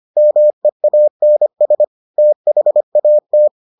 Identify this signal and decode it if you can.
Morse code, heard as a keyed tone, about 25 wpm, 600 Hz. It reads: MEANS THAT